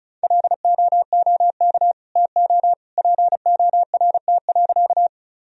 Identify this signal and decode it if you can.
Morse code, heard as a keyed tone, about 34 wpm, 700 Hz. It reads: LOOK TO PORT.
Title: Sentences from Top 1000 Words - Review Part 1 - 35wpm